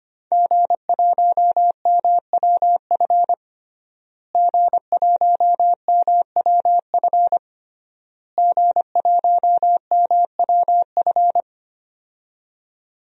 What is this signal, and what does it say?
Morse code, heard as a keyed tone, about 25 wpm, 700 Hz. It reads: G1MWF G1MWF G1MWF
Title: DXCC-100 Call Signs - 5 Characters - Repeated 3x - 25wpm